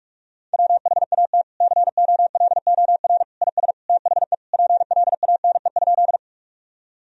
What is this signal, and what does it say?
Morse code, heard as a keyed tone, about 46 wpm, 700 Hz. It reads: WHAT COLOR IS THE PLANE?
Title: Sentences from Top 1000 Words - Review Part 3 - 45wpm